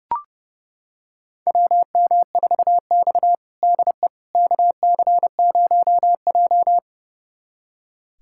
Morse code, heard as a keyed tone, about 30 wpm, 700 Hz. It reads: WM4X DE KC0J
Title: USA Callsign DE Callsign - 4 Characters - 30wpm